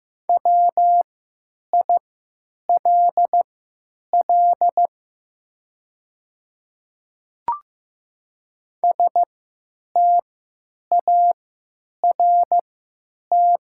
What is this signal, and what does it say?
Morse code, heard as a keyed tone, about 16 wpm, spaced as for 9 wpm, 700 Hz. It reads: WILL START
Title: 3x Character Spacing - Top 300 Words - 15wpm